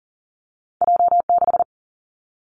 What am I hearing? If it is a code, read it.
Morse code, heard as a keyed tone, about 39 wpm, 700 Hz. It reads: J 6
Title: Two Challenging Letter-Number Combinations 40wpm